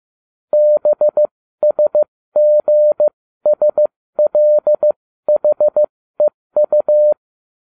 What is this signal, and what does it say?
Morse code, heard as a keyed tone, about 15 wpm, spaced as for 13 wpm, 600 Hz. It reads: BSGSLHEU